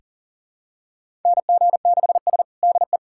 Morse code, heard as a keyed tone, about 37 wpm, 700 Hz. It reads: NG6S DE
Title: USA Callsign DE Callsign - 4 Characters - 40wpm